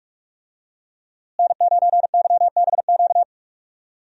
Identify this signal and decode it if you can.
Morse code, heard as a keyed tone, about 46 wpm, 700 Hz. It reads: N9YBQ
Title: USA Call Signs - 5 Characters - Repeated 3x - 45wpm